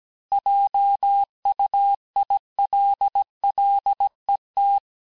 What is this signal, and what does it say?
Morse code, heard as a keyed tone, about 17 wpm, 800 Hz. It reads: JUILLET
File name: cours-du-lundi files / cours-1587-texte-clair-vitesse-16mpm-puis-21mpm